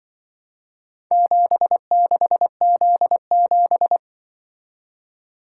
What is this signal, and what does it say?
Morse code, heard as a keyed tone, about 24 wpm, 700 Hz. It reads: 76Z7
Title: Speed-Racing - Four Letter-Number Combinations - 20wpm